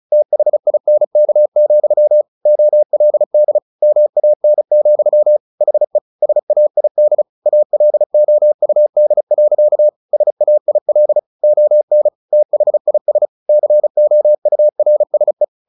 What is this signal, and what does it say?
Morse code, heard as a keyed tone, about 34 wpm, 600 Hz. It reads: THINK, OLD MAN, HE SAID ALOUD. SAIL ON THIS COURSE
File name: the old man and the sea 2 35wpm